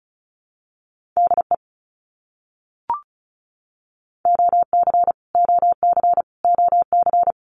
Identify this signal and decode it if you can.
Morse code, heard as a keyed tone, about 34 wpm, 700 Hz. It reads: DE OC OC OC
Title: Abbreviations Repeated 3x - 35wpm